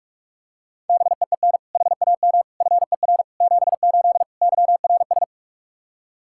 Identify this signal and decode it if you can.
Morse code, heard as a keyed tone, about 46 wpm, 700 Hz. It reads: BEEN HAM FER 78 YRS
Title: Practice QSO - Exchange 3 - More Info - 45wpm